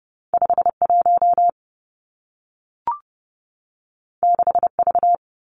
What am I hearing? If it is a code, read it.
Morse code, heard as a keyed tone, about 30 wpm, 700 Hz. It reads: 5 1 6 V
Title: Two Challenging Letter-Number Combinations 30wpm